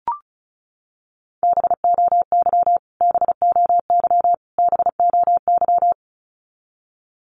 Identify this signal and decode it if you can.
Morse code, heard as a keyed tone, about 34 wpm, 700 Hz. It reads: BOY BOY BOY
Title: Top 200 Words Repeated 35wpm